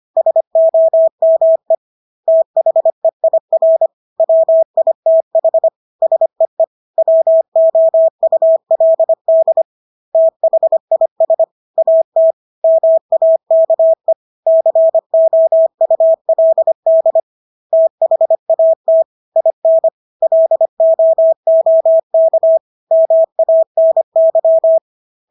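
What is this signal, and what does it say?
Morse code, heard as a keyed tone, about 25 wpm, 650 Hz. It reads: SOME THEIR WITH SEE WOULD THIS AT MAKE COULD THAT IN LOOK MANY
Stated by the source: Morse practice of English words at 25 wpm from CW med Gnister.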